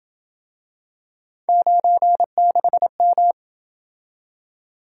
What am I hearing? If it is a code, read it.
Morse code, heard as a keyed tone, about 28 wpm, 700 Hz. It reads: 96M